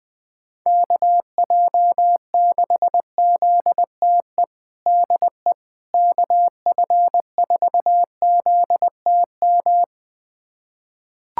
Morse code, heard as a keyed tone, about 21 wpm, 700 Hz. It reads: KJ6ZTE DE KF4ZTM
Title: USA Callsign DE Callsign - 6 Characters - 20wpm